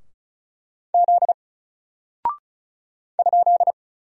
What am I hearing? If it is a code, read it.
Morse code, heard as a keyed tone, about 36 wpm, 700 Hz. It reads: Z ?
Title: Single Character (Letter:Number:Punctuation:Prosign) 35wpm